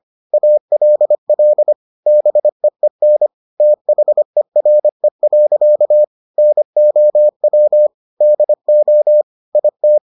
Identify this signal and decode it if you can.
Morse code, heard as a keyed tone, about 25 wpm, 600 Hz. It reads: ALL BEEN THERE. NOW DO IT